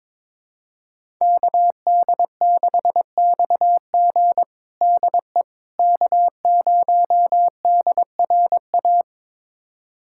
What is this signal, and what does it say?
Morse code, heard as a keyed tone, about 22 wpm, 700 Hz. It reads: KD6XG DE K0DRA
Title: USA Callsign DE Callsign - 5 Characters - 22wpm